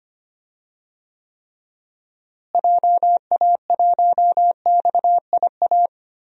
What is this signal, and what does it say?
Morse code, heard as a keyed tone, about 25 wpm, 700 Hz. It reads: JA1XIA
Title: DXCC-100 Call Signs - 6 Characters - Repeated 2x - 25wpm